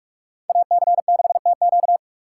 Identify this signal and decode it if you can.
Morse code, heard as a keyed tone, about 59 wpm, 700 Hz. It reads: AC6TQ